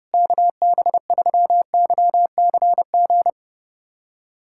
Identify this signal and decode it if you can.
Morse code, heard as a keyed tone, about 30 wpm, 700 Hz. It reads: KB3YCG
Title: USA Callsign DE Callsign - 6 Characters - 30wpm